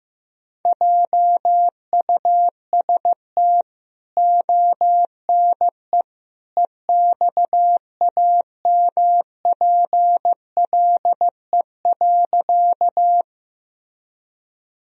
Morse code, heard as a keyed tone, about 15 wpm, 700 Hz. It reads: JUST ONE EXAMPLE.
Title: Sentences from Top 1000 Words - Review Part 1 - 15wpm